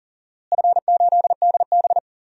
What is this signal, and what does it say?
Morse code, heard as a keyed tone, about 40 wpm, 700 Hz. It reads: F8DB
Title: DXCC-100 Call Signs - 4 Characters - Repeated 3x - 40wpm